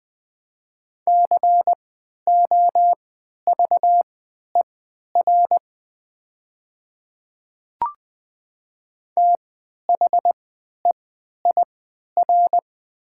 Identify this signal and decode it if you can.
Morse code, heard as a keyed tone, about 20 wpm, spaced as for 11 wpm, 700 Hz. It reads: COVER THEIR
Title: 3x Character Spacing - Top 300 Words - 20wpm